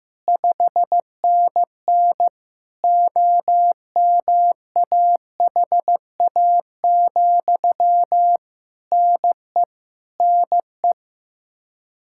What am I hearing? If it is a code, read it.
Morse code, heard as a keyed tone, about 15 wpm, 700 Hz. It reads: 5NN OMAHA, NE NE
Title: Practice QSO - Exchange 1 - Beginning - 15wpm